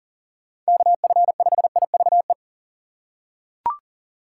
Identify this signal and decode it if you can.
Morse code, heard as a keyed tone, about 39 wpm, 700 Hz. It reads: KF5IVE
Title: USA Call Signs - 6x Character Spacing - 40wpm